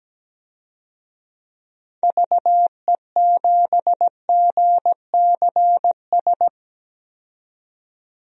Morse code, heard as a keyed tone, about 17 wpm, 700 Hz. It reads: VE7GCS
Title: DXCC-100 Call Signs - 6 Characters - 17wpm